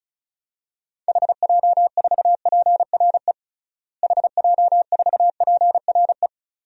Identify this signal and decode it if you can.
Morse code, heard as a keyed tone, about 36 wpm, 700 Hz. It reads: HJ4PRE HJ4PRE
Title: DXCC-100 Call Signs - 6 Characters - Repeated 2x - 35wpm